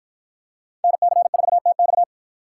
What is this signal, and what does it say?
Morse code, heard as a keyed tone, about 53 wpm, 700 Hz. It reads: NC4TX